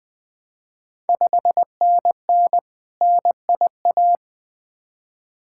Morse code, heard as a keyed tone, about 20 wpm, 700 Hz. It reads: 5NN NIA
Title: Ontario QSO Party Exchange 20wpm